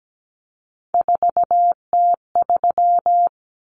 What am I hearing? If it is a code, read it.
Morse code, heard as a keyed tone, about 17 wpm, 700 Hz. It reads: 4T3
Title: Common Cut Numbers - 3-Digit 17wpm